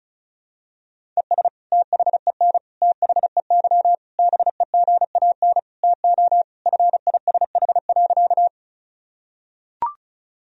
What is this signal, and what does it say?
Morse code, heard as a keyed tone, about 34 wpm, 700 Hz. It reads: ES THEN THEY BEGAN TO FISH.
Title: Sentences from Top 300 Words and Abbreviations 35wpm